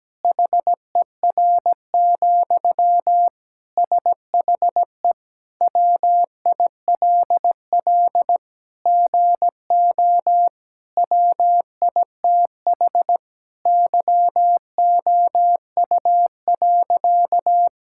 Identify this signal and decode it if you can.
Morse code, heard as a keyed tone, about 17 wpm, 700 Hz. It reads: HER, SHE WILL GO WITH YOU.
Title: Sentences from Top 1600 Words 17wpm